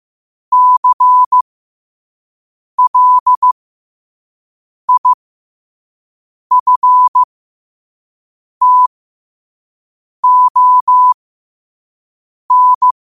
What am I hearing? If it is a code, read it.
Morse code, heard as a keyed tone, about 15 wpm, spaced as for 5 wpm, 1000 Hz. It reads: CLIFTON
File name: 5wpm